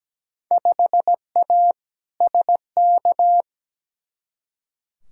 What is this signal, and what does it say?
Morse code, heard as a keyed tone, about 17 wpm, 700 Hz. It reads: 5A SK
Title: Field Day Exchanges Encore - 17wpm